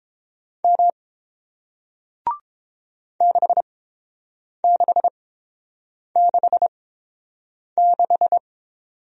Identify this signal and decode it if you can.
Morse code, heard as a keyed tone, about 30 wpm, spaced as for 7 wpm, 700 Hz. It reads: M 6666